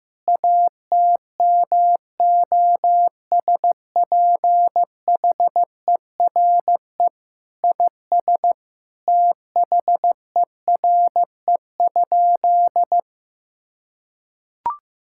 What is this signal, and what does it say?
Morse code, heard as a keyed tone, about 15 wpm, 700 Hz. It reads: ATMOSPHERE IS THERE?
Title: Sentences from Top 1600 Words 15wpm